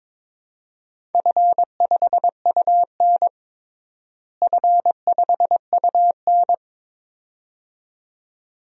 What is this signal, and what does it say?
Morse code, heard as a keyed tone, about 22 wpm, 700 Hz. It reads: F5UN F5UN
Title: DXCC-100 Call Signs - 4 Characters - Repeated 2x - 22wpm